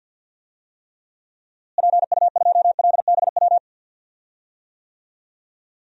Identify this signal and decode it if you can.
Morse code, heard as a keyed tone, about 46 wpm, 700 Hz. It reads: PU2LBW
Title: DXCC-100 Call Signs - 6 Characters - Repeated 3x - 50wpm